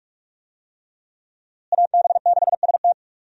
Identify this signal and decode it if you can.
Morse code, heard as a keyed tone, about 46 wpm, 700 Hz. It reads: AB6ST
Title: USA Call Signs - 5 Characters - Repeated 3x - 45wpm